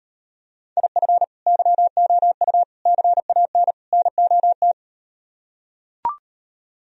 Morse code, heard as a keyed tone, about 38 wpm, 700 Hz. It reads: IF YOU CAN NOT